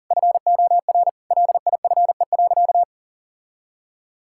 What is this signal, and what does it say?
Morse code, heard as a keyed tone, about 40 wpm, 700 Hz. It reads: FOR LIFE.
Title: Sentences from Top 1000 Words - Review Part 6 - 40wpm